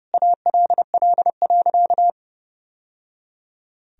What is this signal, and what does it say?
Morse code, heard as a keyed tone, about 30 wpm, 700 Hz. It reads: ALL.
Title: Sentences from Top 1000 Words - Review Part 1 - 30wpm